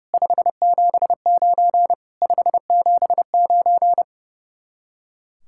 Morse code, heard as a keyed tone, about 36 wpm, 700 Hz. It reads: 579 579
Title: Speed-Racing Common QSO Phrases Encore - 30wpm